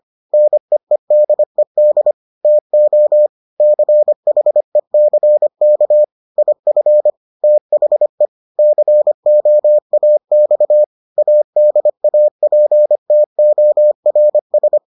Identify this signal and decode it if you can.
Morse code, heard as a keyed tone, about 25 wpm, 600 Hz. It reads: NEEDED TO CHECK IF THE COAX ADAPTORS